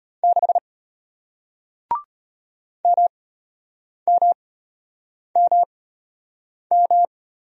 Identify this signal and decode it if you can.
Morse code, heard as a keyed tone, about 33 wpm, spaced as for 8 wpm, 700 Hz. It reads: 6 MMMM